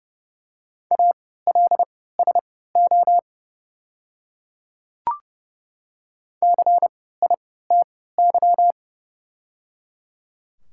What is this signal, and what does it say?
Morse code, heard as a keyed tone, about 30 wpm, spaced as for 17 wpm, 700 Hz. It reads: ALSO CITY
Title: 3x Character Spacing - Top 500 Words - 30wpm